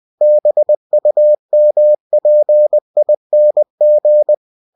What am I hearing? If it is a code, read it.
Morse code, heard as a keyed tone, about 20 wpm, 600 Hz. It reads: BUMPING